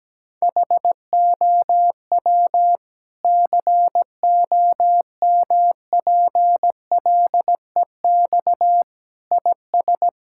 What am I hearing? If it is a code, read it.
Morse code, heard as a keyed tone, about 17 wpm, 700 Hz. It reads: HOW COMPLEX IS
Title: Sentences from Top 1000 Words - Review Part 6 - 17wpm